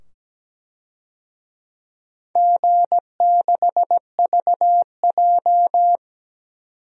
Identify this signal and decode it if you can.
Morse code, heard as a keyed tone, about 17 wpm, 700 Hz. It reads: G6VJ
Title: DXCC-100 Call Signs - 4 Characters - 17wpm